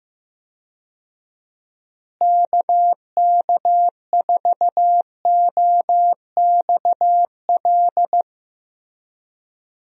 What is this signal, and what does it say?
Morse code, heard as a keyed tone, about 15 wpm, 700 Hz. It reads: KK4OXL
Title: USA Call Signs - 6 Characters - Repeated 2x - 15wpm